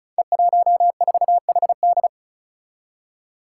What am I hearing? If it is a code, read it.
Morse code, heard as a keyed tone, about 36 wpm, 700 Hz. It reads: E14HD
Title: Five Challenging Letter-Number Combinations 35wpm